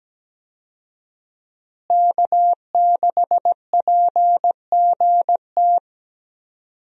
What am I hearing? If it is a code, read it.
Morse code, heard as a keyed tone, about 17 wpm, 700 Hz. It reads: K6PGT
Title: USA Call Signs - 5 Characters - Repeated 3x - 17wpm